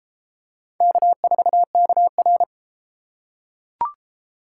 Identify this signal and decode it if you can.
Morse code, heard as a keyed tone, about 29 wpm, 700 Hz. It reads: K4KR